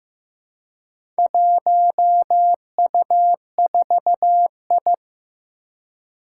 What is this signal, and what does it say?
Morse code, heard as a keyed tone, about 15 wpm, 700 Hz. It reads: 1U4I
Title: Four Challenging Letter-Number Combinations 15wpm